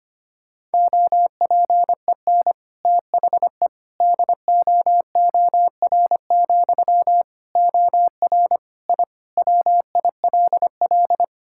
Morse code, heard as a keyed tone, about 25 wpm, 700 Hz. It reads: OPEN THE DOOR, OR I WILL
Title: Sentences from Top 1000 Words - Review Part 6 - 25wpm